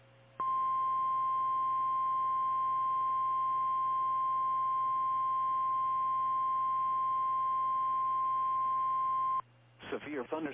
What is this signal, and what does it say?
Emergency Alert System (SAME) tones (the 1050 Hz attention tone), with speech. It is heard from 0.4 to 9.5 s.
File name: eas / KKPRKXPN / KKPRKXPN